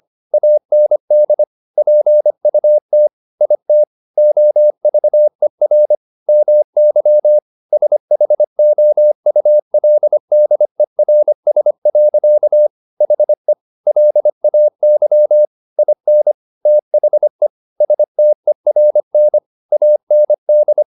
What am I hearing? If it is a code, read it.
Morse code, heard as a keyed tone, about 25 wpm, 600 Hz. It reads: AND PUT IT OVER MY SHOULDERS. HE LAY IN THE STERN AND